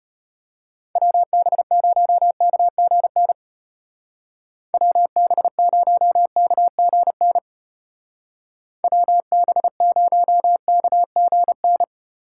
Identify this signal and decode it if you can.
Morse code, heard as a keyed tone, about 31 wpm, 700 Hz. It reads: WB0KGN WB0KGN WB0KGN
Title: Speed Racing - USA Call Signs 25wpm